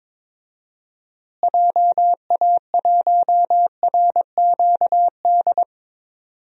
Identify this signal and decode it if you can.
Morse code, heard as a keyed tone, about 22 wpm, 700 Hz. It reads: JA1RQD
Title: DXCC-100 Call Signs - 6 Characters - Repeated 3x - 22wpm